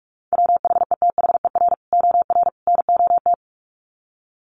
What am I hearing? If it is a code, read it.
Morse code, heard as a keyed tone, about 44 wpm, 700 Hz. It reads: WHETHER OR NOT
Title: Sets of 3 Words 45wpm